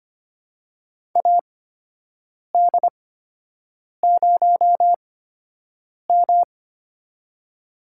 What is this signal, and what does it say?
Morse code, heard as a keyed tone, about 25 wpm, spaced as for 7 wpm, 700 Hz. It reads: AD0M
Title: USA Call Signs - 8x Character Spacing - 25wpm